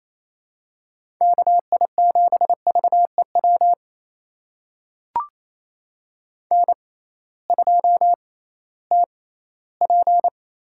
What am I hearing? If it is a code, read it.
Morse code, heard as a keyed tone, about 28 wpm, spaced as for 16 wpm, 700 Hz. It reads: KI7VEW N2TP
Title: USA Call Signs - 6x Character Spacing - 28wpm